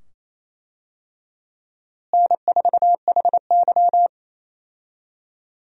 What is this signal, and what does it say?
Morse code, heard as a keyed tone, about 27 wpm, 700 Hz. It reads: N4HY
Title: USA Call Signs - 4 Characters - Repeated 2x - 28wpm